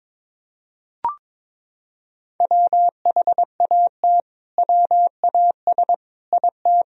Morse code, heard as a keyed tone, about 22 wpm, 700 Hz. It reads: WHAT WAS IT
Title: Sentences from Top 100 Words and Abbreviations 22wpm